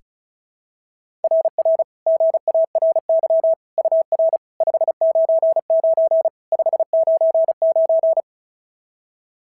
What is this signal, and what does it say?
Morse code, heard as a keyed tone, about 36 wpm, 690 Hz. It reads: RR GARY UR 599 599
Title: SOTA QSO Practice 35wpm